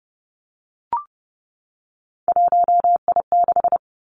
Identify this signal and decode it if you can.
Morse code, heard as a keyed tone, about 30 wpm, 700 Hz. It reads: 1I6
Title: Three Challenging Letter-Number Combinations 30wpm